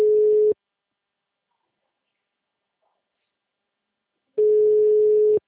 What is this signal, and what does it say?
Ringback tone (Europe, 425 Hz).
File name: out- -2002- - - .47